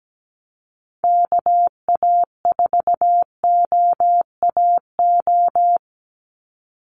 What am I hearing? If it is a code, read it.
Morse code, heard as a keyed tone, about 17 wpm, 700 Hz. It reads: KA4OAO
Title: USA Call Signs 17wpm